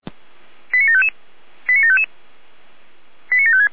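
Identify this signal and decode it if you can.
Five-tone selective call (ZVEI), heard as quick runs of five tones, tones of about 70 ms. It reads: 89755 89755 89756
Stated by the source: ZVEI five-tone selective call received by SDR.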